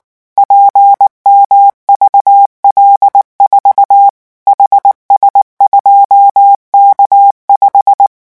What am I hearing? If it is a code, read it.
Morse code, heard as a keyed tone, about 19 wpm, 800 Hz. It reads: PMVL4 HS2K5
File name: morse mp3 / revision12-vitesse 18 mots